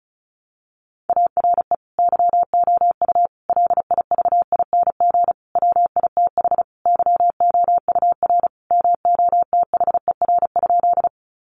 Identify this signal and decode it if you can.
Morse code, heard as a keyed tone, about 34 wpm, 700 Hz. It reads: ARE YOU LIVING WITH YOUR MOTHER?
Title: Sentences from Top 1100 Words 35wpm